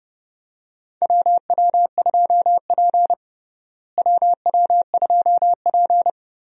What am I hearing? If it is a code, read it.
Morse code, heard as a keyed tone, about 30 wpm, 700 Hz. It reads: WW2P WW2P
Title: USA Call Signs - 4 Characters - Repeated 2x - 30wpm